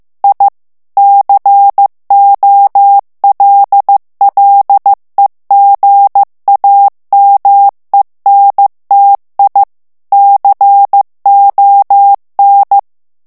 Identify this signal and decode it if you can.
Morse code, heard as a keyed tone, about 15 wpm, 800 Hz. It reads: I COLLEGAMENTI CON
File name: ITALIEN texte clair 1 vitesse 14 mots minute